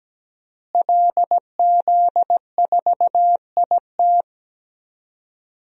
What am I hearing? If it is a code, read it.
Morse code, heard as a keyed tone, about 17 wpm, 700 Hz. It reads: LZ4IT
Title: DXCC-100 Call Signs - 5 Characters - Repeated 2x - 17wpm